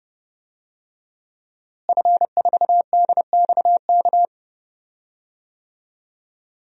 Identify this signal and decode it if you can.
Morse code, heard as a keyed tone, about 30 wpm, 700 Hz. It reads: F4DXK
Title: DXCC-100 Call Signs - 5 Characters - Repeated 2x - 30wpm